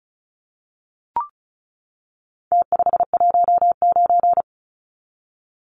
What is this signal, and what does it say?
Morse code, heard as a keyed tone, about 34 wpm, 700 Hz. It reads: T519
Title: Numbers - Contest Style - 4-Digit 35wpm